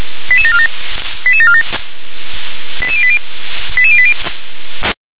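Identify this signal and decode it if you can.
Five-tone selective call (ZVEI), heard as quick runs of five tones, tones of about 70 ms. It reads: 88747 88747 88080 88080
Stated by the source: ZVEI five-tone selective call received with an SDR.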